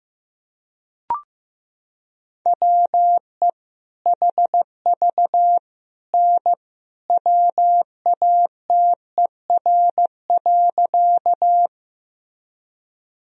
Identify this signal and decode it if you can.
Morse code, heard as a keyed tone, about 15 wpm, 700 Hz. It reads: WE HV N WATER.
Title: Sentences from Top 100 Words and Abbreviations 15wpm